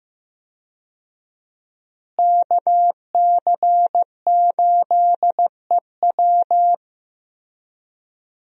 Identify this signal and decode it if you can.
Morse code, heard as a keyed tone, about 15 wpm, 700 Hz. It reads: KC8EW
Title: USA Call Signs - 5 Characters - Repeated 3x - 15wpm